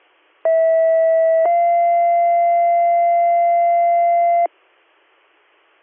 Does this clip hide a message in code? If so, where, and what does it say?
Yes: two-tone page from 0.4 to 4.5 s, heard as a one-second tone, then a three-second tone, about 650 then 690 Hz.